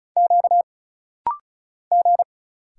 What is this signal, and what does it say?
Morse code, heard as a keyed tone, about 34 wpm, 700 Hz. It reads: Q G
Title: Single Letters - Rapid-Fire 35wpm